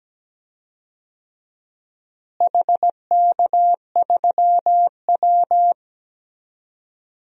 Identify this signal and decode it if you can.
Morse code, heard as a keyed tone, about 17 wpm, 700 Hz. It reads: HK3W
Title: DXCC-100 Call Signs - 4 Characters - Repeated 3x - 17wpm